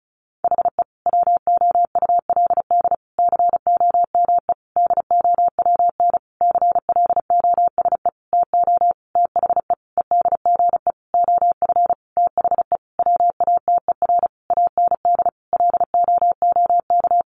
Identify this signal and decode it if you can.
Morse code, heard as a keyed tone, about 34 wpm, 700 Hz. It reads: HE WOULD COME DOWN CLOSE TO THE EDGE OF THE WATER AND LOOK
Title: Sentences from Top 1300 Words 35wpm